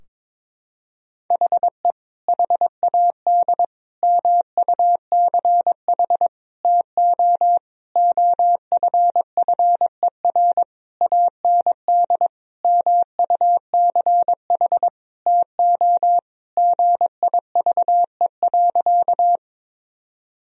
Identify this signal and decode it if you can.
Morse code, heard as a keyed tone, about 22 wpm, 700 Hz. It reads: HE HAD MUCH TO OFFER AND MUCH TO GIVE.
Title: Sentences from Top 1000 Words - Review Part 4 - 22wpm